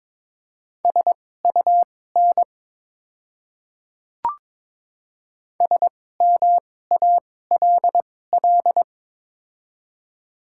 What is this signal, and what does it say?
Morse code, heard as a keyed tone, about 22 wpm, spaced as for 16 wpm, 700 Hz. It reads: SUN SMALL
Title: 2x Character Spacing - Top 300 Words - 22wpm